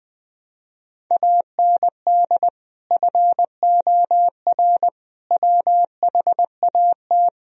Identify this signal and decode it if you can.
Morse code, heard as a keyed tone, about 21 wpm, 700 Hz. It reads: AND FOR WHAT
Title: Sentences from Top 1000 Words - Review Part 1 - 20wpm